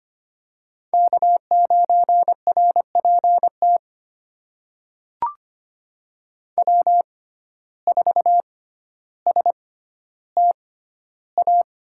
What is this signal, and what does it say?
Morse code, heard as a keyed tone, about 25 wpm, spaced as for 13 wpm, 700 Hz. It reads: K9RPT W4STA
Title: USA Call Signs - 5 Characters - 6x Character Spacing - 25wpm